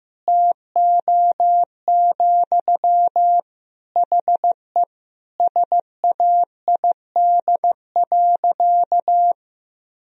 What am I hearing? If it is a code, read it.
Morse code, heard as a keyed tone, about 15 wpm, 700 Hz. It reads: TO, HE SAID.
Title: Sentences from Top 1000 Words - Review Part 2 - 15wpm